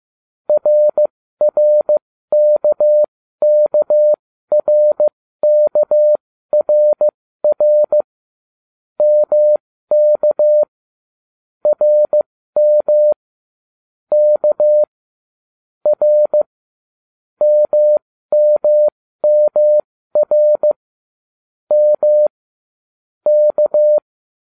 Morse code, heard as a keyed tone, about 15 wpm, spaced as for 13 wpm, 600 Hz. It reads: RRKKRKRR MK RM K R MMMR M K